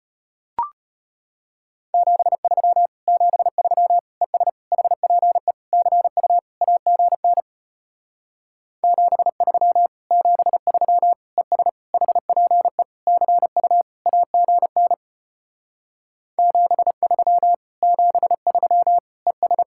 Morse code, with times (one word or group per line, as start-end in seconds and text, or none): 1.9-2.9 73
3.1-4.0 73
4.2-4.5 ES
4.7-5.5 HPE
5.7-6.4 CU
6.6-7.4 AGN
8.8-9.9 73
10.1-11.1 73
11.4-11.7 ES
11.9-12.8 HPE
13.1-13.8 CU
14.1-14.9 AGN
16.4-17.5 73
17.8-19.0 73
19.3-19.6 ES